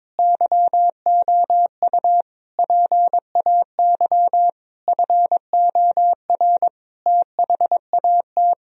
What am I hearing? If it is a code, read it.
Morse code, heard as a keyed tone, about 22 wpm, 700 Hz. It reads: YOU PAY FOR THAT
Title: Sentences from Top 1000 Words - Review Part 3 - 22wpm